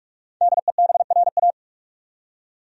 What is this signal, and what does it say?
Morse code, heard as a keyed tone, about 46 wpm, 700 Hz. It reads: DEBRA
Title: USA Common Names - 45wpm